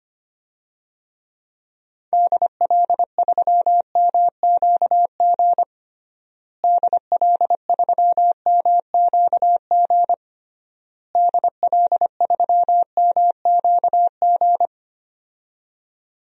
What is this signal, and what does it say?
Morse code, heard as a keyed tone, about 25 wpm, 700 Hz. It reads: DL3MQG DL3MQG DL3MQG
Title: DXCC-100 Call Signs - 6 Characters - Repeated 3x - 25wpm